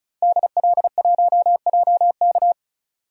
Morse code, heard as a keyed tone, about 34 wpm, 700 Hz. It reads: DL1JK
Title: DXCC-100 Call Signs - 5 Characters - 35wpm